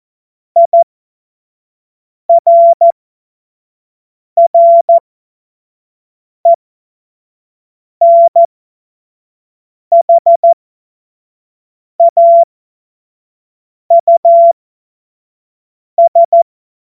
Morse code, heard as a keyed tone, about 14 wpm, spaced as for 5 wpm, 680 Hz. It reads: IRRENHAUS
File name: Morsekurs / Witze / Witz 29 April 2025 25 F70